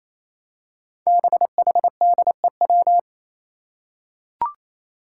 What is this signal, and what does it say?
Morse code, heard as a keyed tone, about 28 wpm, 700 Hz. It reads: BHDEW
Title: Five Challenging Letter-Number Combinations 28wpm